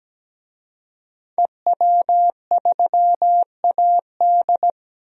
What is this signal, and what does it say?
Morse code, heard as a keyed tone, about 17 wpm, 700 Hz. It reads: EW3AD
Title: DXCC-100 Call Signs - 5 Characters - 17wpm